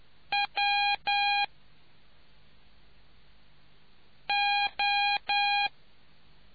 Morse code, heard as a keyed tone, about 10 wpm, 810 Hz. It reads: W O